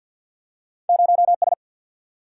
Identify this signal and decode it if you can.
Morse code, heard as a keyed tone, about 50 wpm, 700 Hz. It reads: 0 S
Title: Single-Letter Single-Number 50wpm